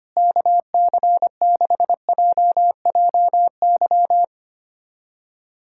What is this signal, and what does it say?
Morse code, heard as a keyed tone, about 25 wpm, 700 Hz. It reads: KC6JJY
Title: USA Callsign DE Callsign 25wpm